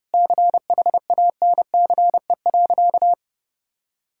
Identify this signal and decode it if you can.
Morse code, heard as a keyed tone, about 32 wpm, 700 Hz. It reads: CHANCE.
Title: Sentences from Top 1000 Words - Review Part 6 - 30wpm